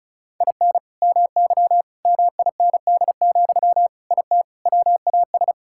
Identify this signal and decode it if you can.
Morse code, heard as a keyed tone, about 36 wpm, 700 Hz. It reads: IN MY MIND, IT WAS
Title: Sentences from Top 1000 Words - Review Part 6 - 35wpm